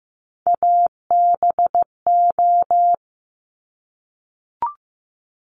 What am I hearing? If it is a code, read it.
Morse code, heard as a keyed tone, about 15 wpm, 700 Hz. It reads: ABO
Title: Top Three Letter Combinations in Words 15wpm